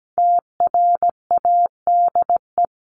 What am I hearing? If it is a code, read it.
Morse code, heard as a keyed tone, about 17 wpm, 700 Hz. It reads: TRADE
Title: Top 501-750 Words 17wpm